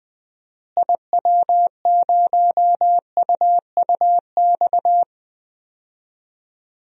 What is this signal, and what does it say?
Morse code, heard as a keyed tone, about 20 wpm, 700 Hz. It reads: IW0UUX
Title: DXCC-100 Call Signs - 6 Characters - Repeated 2x - 20wpm